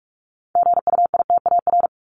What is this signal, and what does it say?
Morse code, heard as a keyed tone, about 44 wpm, 700 Hz. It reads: GUITAR
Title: Top 1501-1750 Words 45wpm